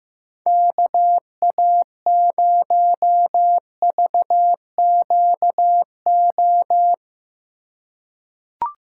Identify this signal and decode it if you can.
Morse code, heard as a keyed tone, about 15 wpm, 700 Hz. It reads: KA0VQO
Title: USA Call Signs - 6 Characters - 4x Character Spacing - 15wpm